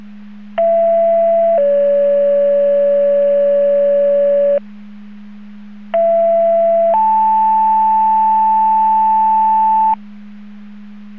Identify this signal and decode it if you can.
Two-tone page, heard as a one-second tone, then a three-second tone, about 670 then 570 Hz, about 670 then 880 Hz. It is heard from 0.6 to 9.9 s.